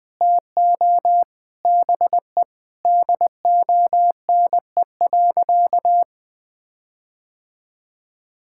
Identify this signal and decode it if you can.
Morse code, heard as a keyed tone, about 21 wpm, 700 Hz. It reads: TO BE DONE.
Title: Sentences with 9-Word Phrases - Spoken Beginning - 20wpm